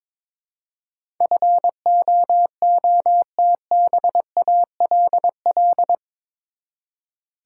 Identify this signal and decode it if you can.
Morse code, heard as a keyed tone, about 22 wpm, 700 Hz. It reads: FOOTBALL
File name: practice / 22wpm / Eight Letter Words - 22wpm